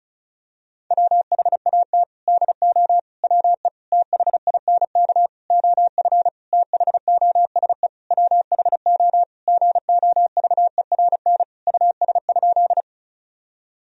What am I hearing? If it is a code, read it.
Morse code, heard as a keyed tone, about 36 wpm, 700 Hz. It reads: WHAT DO WE THINK OF THOSE WHO GOVERN US?
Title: Sentences from Top 1000 Words - Review Part 2 - 35wpm